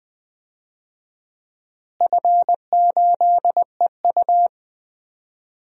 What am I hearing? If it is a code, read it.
Morse code, heard as a keyed tone, about 21 wpm, 700 Hz. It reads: F8EU
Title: DXCC-100 Call Signs - 4 Characters - Repeated 2x - 20wpm